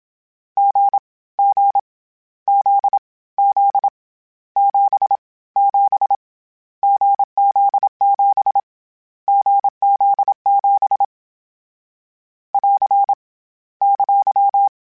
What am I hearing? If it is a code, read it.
Morse code, heard as a keyed tone, about 27 wpm, 800 Hz. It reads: G G Z Z 7 7 GZ7 GZ7 <AR> !
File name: Difficult Combos / Difficult Combos-mp3s / awj1 ndb6 gz7 @25